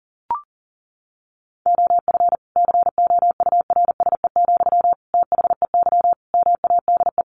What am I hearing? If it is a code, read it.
Morse code, heard as a keyed tone, about 40 wpm, 700 Hz. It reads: OF COURSE, THEY MADE
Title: Sentences from Top 1400 Words 40wpm